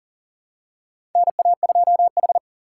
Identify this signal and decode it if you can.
Morse code, heard as a keyed tone, about 40 wpm, 700 Hz. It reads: NA2H
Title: USA Call Signs - 3x Character Spacing - 40wpm